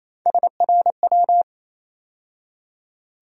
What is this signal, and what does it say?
Morse code, heard as a keyed tone, about 28 wpm, 700 Hz. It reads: SRW
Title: USA Call Sign Postfixes - 3 Letters 28wpm